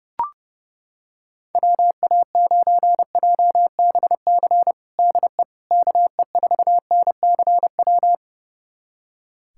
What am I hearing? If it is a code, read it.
Morse code, heard as a keyed tone, about 32 wpm, 700 Hz. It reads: WA9JBC DE KE4NCW
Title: USA Callsign DE Callsign - 6 Characters - 30wpm